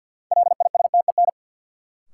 Morse code, heard as a keyed tone, about 53 wpm, 700 Hz. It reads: LISTEN
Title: Sentences from Top 400 Words - New Word Drill - 50wpm